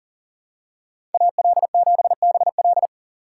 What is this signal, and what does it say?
Morse code, heard as a keyed tone, about 40 wpm, 700 Hz. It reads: AL7BL
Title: USA Call Signs - 5 Characters - 2x Character Spacing - 40wpm